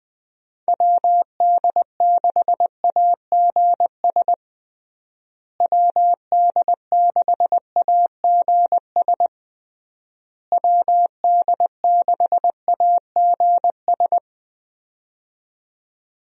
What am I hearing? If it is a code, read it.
Morse code, heard as a keyed tone, about 20 wpm, 700 Hz. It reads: WD6AGS WD6AGS WD6AGS
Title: USA Call Signs - 6 Characters - Repeated 3x - 20wpm